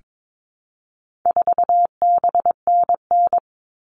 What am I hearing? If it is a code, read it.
Morse code, heard as a keyed tone, about 22 wpm, 700 Hz. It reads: 4BNN
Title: All Cut Numbers - 4-Digit 22wpm